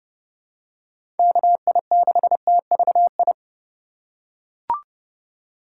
Morse code, heard as a keyed tone, about 32 wpm, 700 Hz. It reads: KI6TVI
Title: USA Call Signs - 6 Characters - 4x Character Spacing - 30wpm